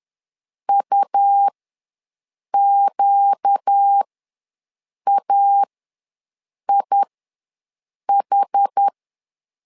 Morse code, heard as a keyed tone, about 10 wpm, spaced as for 6 wpm, 800 Hz. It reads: UQAIH
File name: LEZIONI / 4 Q1 ws 10wpm 2538